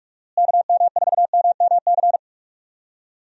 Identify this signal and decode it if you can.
Morse code, heard as a keyed tone, about 46 wpm, 700 Hz. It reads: KM4MMC
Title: USA Call Signs - 6 Characters - Repeated 2x - 45wpm